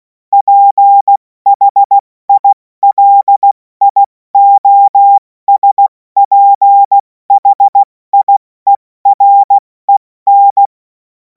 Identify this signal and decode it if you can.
Morse code, heard as a keyed tone, about 16 wpm, 800 Hz. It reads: PHILIOSPHIEREN
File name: Morsekurs / Witze / Witz 24. Februar 2026 70 F80